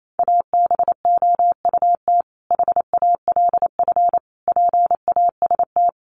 Morse code, heard as a keyed tone, about 28 wpm, 700 Hz. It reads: ABOUT HALF PAST
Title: Sentences from Top 600 Words 28wpm